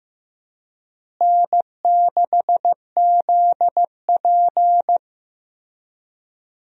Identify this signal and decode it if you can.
Morse code, heard as a keyed tone, about 15 wpm, 700 Hz. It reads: N6ZP
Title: USA Call Signs - 4 Characters - Repeated 3x - 15wpm